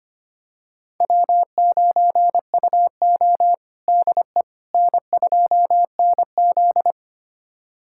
Morse code, heard as a keyed tone, about 25 wpm, 700 Hz. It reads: W9UO DE N2NZ
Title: USA Callsign DE Callsign - 4 Characters - 25wpm